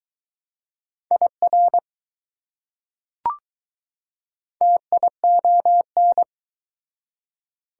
Morse code, heard as a keyed tone, about 20 wpm, 700 Hz. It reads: IR TION